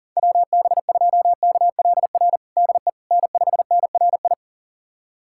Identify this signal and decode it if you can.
Morse code, heard as a keyed tone, about 40 wpm, 700 Hz. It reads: WB2KLR DE N5NRI
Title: USA Callsign DE Callsign 40wpm